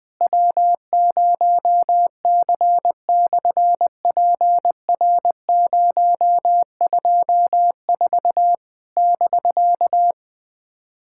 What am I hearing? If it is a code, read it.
Morse code, heard as a keyed tone, about 20 wpm, 700 Hz. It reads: W0C/PR024 <BK>
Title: SOTA QSO Practice 20wpm